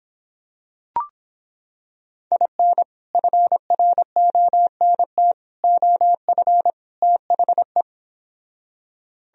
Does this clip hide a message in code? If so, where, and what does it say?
Yes: Morse code from 2.3 to 7.8 s, heard as a keyed tone, about 26 wpm, 700 Hz. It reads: IN FRONT OF THE